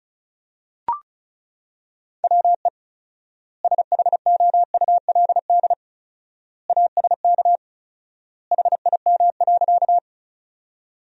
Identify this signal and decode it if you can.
Morse code, heard as a keyed tone, about 34 wpm, 700 Hz. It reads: WE SHOULD ASK HIM.
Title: 4x Word-Spacing - Sentences from Top 300 Words - 35wpm